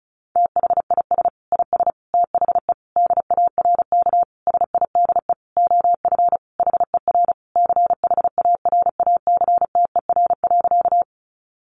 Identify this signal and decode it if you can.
Morse code, heard as a keyed tone, about 34 wpm, 700 Hz. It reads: THIS IS THE DARK SIDE OF HER CHARACTER.
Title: Sentences from Top 1200 Words 35wpm